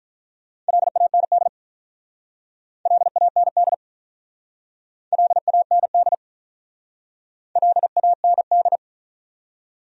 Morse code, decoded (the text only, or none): LAND LAND LAND LAND